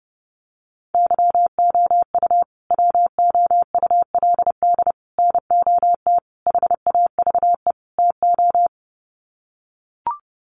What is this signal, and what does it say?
Morse code, heard as a keyed tone, about 30 wpm, 700 Hz. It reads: YOU WOULD NOT HAVE TO
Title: Sets of 5 Words Encore 30wpm